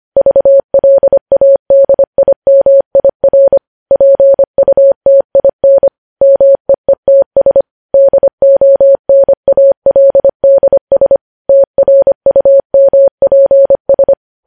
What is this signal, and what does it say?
Morse code, heard as a keyed tone, about 25 wpm, 550 Hz. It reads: VLADIMIR PUTIN MEETS DONALDS TRUMPS
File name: podcast / News headlines-1x-25x25